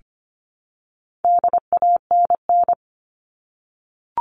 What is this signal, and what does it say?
Morse code, heard as a keyed tone, about 25 wpm, 700 Hz. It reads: DANN
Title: All Cut Numbers - 4-Digit 25wpm